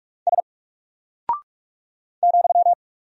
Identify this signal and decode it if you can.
Morse code, heard as a keyed tone, about 44 wpm, 700 Hz. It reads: S ,
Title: Single Character (Letter:Number:Punctuation:Prosign) Alt 45wpm